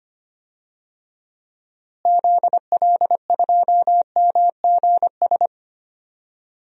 Morse code, heard as a keyed tone, about 25 wpm, 700 Hz. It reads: ZL2MGS
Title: DXCC-100 Call Signs - 6 Characters - Repeated 2x - 25wpm